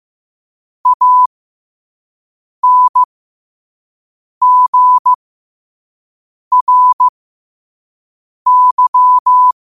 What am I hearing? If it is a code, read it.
Morse code, heard as a keyed tone, about 15 wpm, spaced as for 5 wpm, 1000 Hz. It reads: ANGRY
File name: qotdfiles / 5wpm